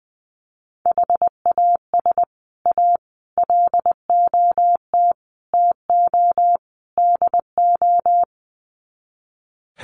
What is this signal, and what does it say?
Morse code, heard as a keyed tone, about 20 wpm, 700 Hz. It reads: HAS A LOT TO DO
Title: Sets of 5 Words Encore2 20wpm